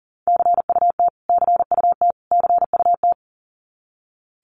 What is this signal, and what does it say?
Morse code, heard as a keyed tone, about 40 wpm, 700 Hz. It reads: CUT CUT CUT
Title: Top 500 Words Repeated 40wpm